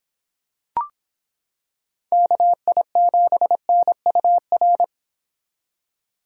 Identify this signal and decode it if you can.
Morse code, heard as a keyed tone, about 26 wpm, 700 Hz. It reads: KI7NUR